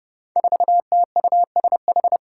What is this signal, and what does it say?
Morse code, heard as a keyed tone, about 32 wpm, 700 Hz. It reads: 4TUSH
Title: Five Challenging Letter-Number Combinations 30wpm